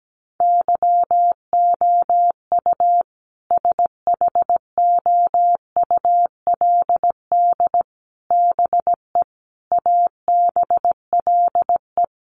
Morse code, heard as a keyed tone, about 17 wpm, 700 Hz. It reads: YOU SHOULD BE ABLE
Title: Sets of 5 Words Encore 17wpm